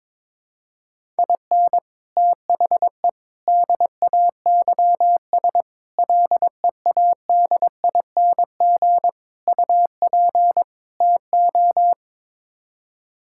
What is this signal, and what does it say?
Morse code, heard as a keyed tone, about 22 wpm, 700 Hz. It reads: IN THE DAYS LEADING UP TO
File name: practice / 22wpm / Sets of 6 Words - 22wpm